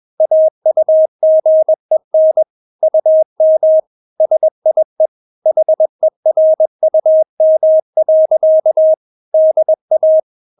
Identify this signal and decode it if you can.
Morse code, heard as a keyed tone, about 21 wpm, 630 Hz. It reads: AUGEN UM SIE HERUM. DA
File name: Witz2 21wpm